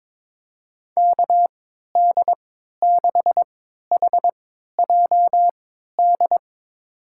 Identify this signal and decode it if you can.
Morse code, heard as a keyed tone, about 22 wpm, spaced as for 12 wpm, 700 Hz. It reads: KD6HJD
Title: USA Call Signs - 6 Characters - 3x Character Spacing - 22wpm